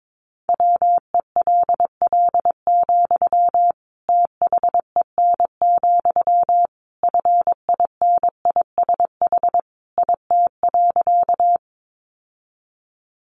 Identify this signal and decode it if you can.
Morse code, heard as a keyed tone, about 22 wpm, 700 Hz. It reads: WELL, THEN, FINISH IT.
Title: Sentences from Top 1200 Words 22wpm